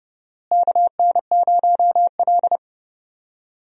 Morse code, heard as a keyed tone, about 31 wpm, 700 Hz. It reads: KN0L
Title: USA Call Signs - 4 Characters - Repeated 3x - 30wpm